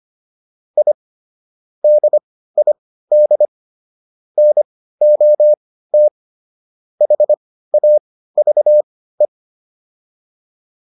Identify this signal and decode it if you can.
Morse code, heard as a keyed tone, about 25 wpm, spaced as for 15 wpm, 600 Hz. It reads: I DID NOT HAVE